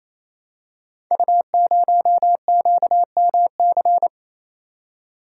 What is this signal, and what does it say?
Morse code, heard as a keyed tone, about 27 wpm, 700 Hz. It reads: U0QMC
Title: Code Groups - 5 Letter:Numbers - 28wpm